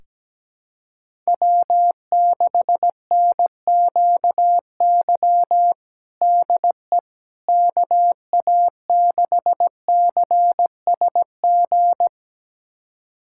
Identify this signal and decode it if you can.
Morse code, heard as a keyed tone, about 17 wpm, 700 Hz. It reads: W6NQY DE KA6CSG
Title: USA Callsign DE Callsign 17wpm